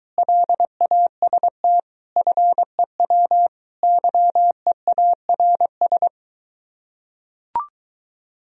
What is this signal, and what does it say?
Morse code, heard as a keyed tone, about 24 wpm, 700 Hz. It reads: LAST FEW YEARS